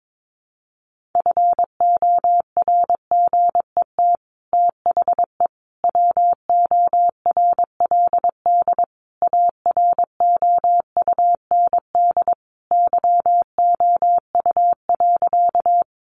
Morse code, heard as a keyed tone, about 22 wpm, 700 Hz. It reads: FORGET THE WORLD AROUND YOU.
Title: Sentences from Top 1200 Words 22wpm